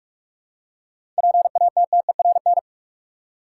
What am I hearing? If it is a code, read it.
Morse code, heard as a keyed tone, about 45 wpm, 700 Hz. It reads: PATTERN
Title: Seven Letter Words - 45wpm